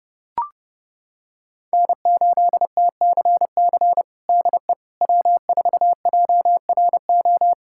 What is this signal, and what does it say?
Morse code, heard as a keyed tone, about 30 wpm, 700 Hz. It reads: N8TCC DE W4JRO
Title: USA Callsign DE Callsign - 5 Characters - 30wpm